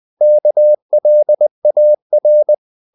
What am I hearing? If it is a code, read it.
Morse code, heard as a keyed tone, about 20 wpm, 600 Hz. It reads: KLAR
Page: Ferie 20 wpm